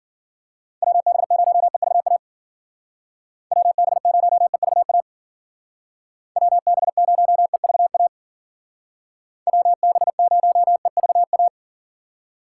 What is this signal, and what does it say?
Morse code, heard as a keyed tone, about 51 wpm, 700 Hz. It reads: WB0EVA WB0EVA WB0EVA WB0EVA